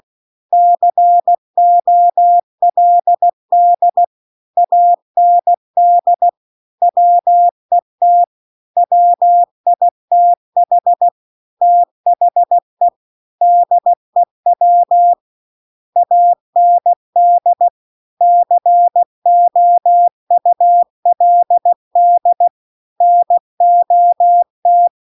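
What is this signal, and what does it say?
Morse code, heard as a keyed tone, about 16 wpm, 700 Hz. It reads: COLD AND WET WITH THE DEW AND COULD NOT
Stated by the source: Morse practice text at 16 WPM.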